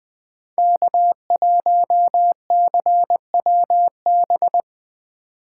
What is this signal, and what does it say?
Morse code, heard as a keyed tone, about 20 wpm, 700 Hz. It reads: K1CWB
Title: 2023 POTA Hunter Call Signs - 20wpm